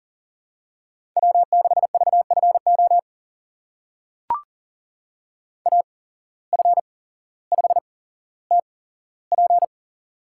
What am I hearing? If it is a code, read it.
Morse code, heard as a keyed tone, about 40 wpm, spaced as for 17 wpm, 700 Hz. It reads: W6VFO AF5TP
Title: USA Call Signs - 5 Characters - 8x Character Spacing - 40wpm